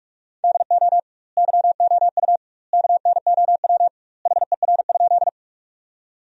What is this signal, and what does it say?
Morse code, heard as a keyed tone, about 46 wpm, 700 Hz. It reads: DO YOU KNOW HER?
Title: 2x Word-Spacing - Sentences from Top 100 Words - 45wpm